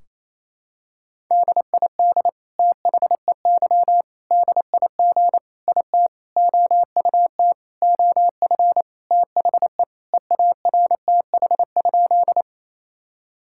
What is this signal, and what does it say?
Morse code, heard as a keyed tone, about 28 wpm, 700 Hz. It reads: DID THEY DIG IT OUT OF THE EARTH?
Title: Sentences from Top 1600 Words 28wpm